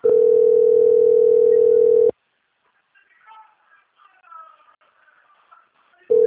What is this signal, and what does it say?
Ringback tone (North America), 440 + 480 Hz.
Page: Dantas - PE